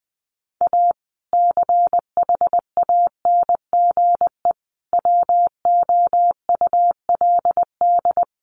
Morse code, heard as a keyed tone, about 20 wpm, 700 Hz. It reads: A CHANGE WOULD
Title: Sentences from Top 1400 Words 20wpm